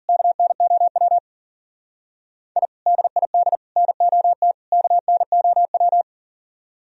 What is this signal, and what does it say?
Morse code, heard as a keyed tone, about 38 wpm, 700 Hz. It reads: KNOW I DID NOT KNOW
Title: Speed-Racing - Sets of 4 Words 40wpm